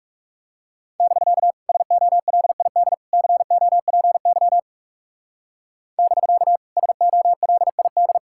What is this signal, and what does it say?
Morse code, heard as a keyed tone, about 38 wpm, 700 Hz. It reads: <BK> SOLID COPY <BK> SOLID